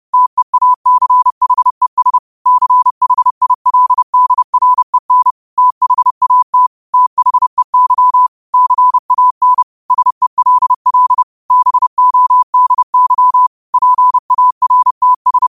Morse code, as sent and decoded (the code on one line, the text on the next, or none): - . .- -.-. .... ... / -.-. .... .. .-.. -.. .-. -. / - .... .- - / - .... -.-- / -.-. .- -. / ... .-.. .-.. / -... --- -.. -.-- / .--. .- .-. - ...
TEACHES CHILDREN THAT THEY CAN SELL BODY PARTS